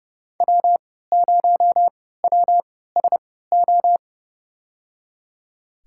Morse code, heard as a keyed tone, about 31 wpm, spaced as for 17 wpm, 700 Hz. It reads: W0WSO
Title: USA Call Signs - 5 Characters - 3x Character Spacing - 30wpm